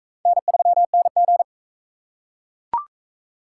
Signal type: Morse code

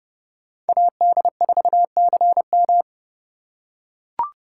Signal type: Morse code

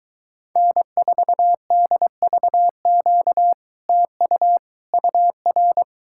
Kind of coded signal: Morse code

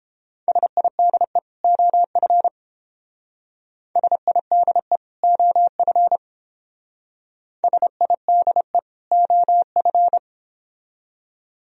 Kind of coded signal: Morse code

